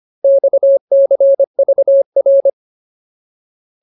Morse code, heard as a keyed tone, about 25 wpm, 550 Hz. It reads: XCVR
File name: Ultimate Word List / 25WPM Ultimate